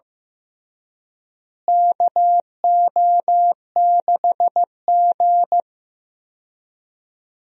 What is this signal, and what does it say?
Morse code, heard as a keyed tone, about 15 wpm, 700 Hz. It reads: KO6G
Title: USA Call Signs - 4 Characters - Repeated 3x - 15wpm